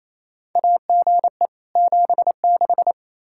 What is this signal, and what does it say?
Morse code, heard as a keyed tone, about 28 wpm, 700 Hz. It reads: AGE 76
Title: K1USN SST QSO Practice 28wpm